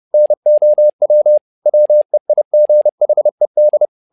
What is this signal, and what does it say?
Morse code, heard as a keyed tone, about 30 wpm, 600 Hz. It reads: NOW WEIGHED
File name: learning center / tower trouble / tower trouble 30wpm